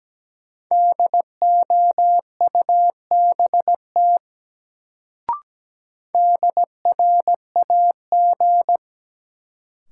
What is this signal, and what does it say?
Morse code, heard as a keyed tone, about 17 wpm, 700 Hz. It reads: DOUBT DRAG
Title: Sentences from Top 1500 Words - New Word Drill Alt - 17wpm